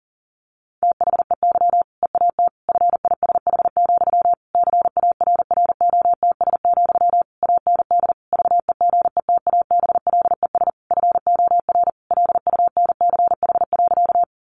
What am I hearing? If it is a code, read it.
Morse code, heard as a keyed tone, about 40 wpm, 700 Hz. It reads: THEY EAT FISH, CARROTS, AND VEGETABLES FOR LUNCH.
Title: Easy Phrases 40wpm